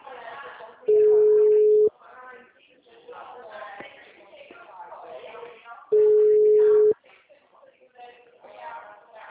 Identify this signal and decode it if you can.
Ringback tone (Europe, 425 Hz), with speech.
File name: out- -2011- - - .55219